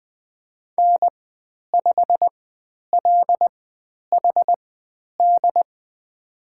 Morse code, heard as a keyed tone, about 20 wpm, spaced as for 10 wpm, 700 Hz. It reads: N5LHD
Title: 109 Lesson 18g - Letter m - Call Signs 20wpm